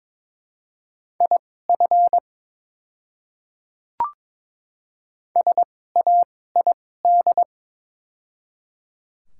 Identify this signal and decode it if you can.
Morse code, heard as a keyed tone, about 22 wpm, spaced as for 16 wpm, 690 Hz. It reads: IF SAID
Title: 2x Character Spacing - Top 100 Words - 22wpm